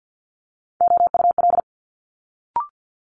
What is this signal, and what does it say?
Morse code, heard as a keyed tone, about 50 wpm, 700 Hz. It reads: OUL
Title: Multi-letter Phonemes Quick Fire 50wpm